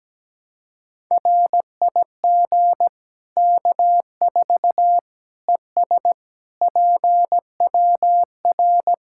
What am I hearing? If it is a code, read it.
Morse code, heard as a keyed tone, about 17 wpm, 700 Hz. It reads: RIG K4 ES PWR
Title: Practice QSO - Exchange 2 - Info - 17wpm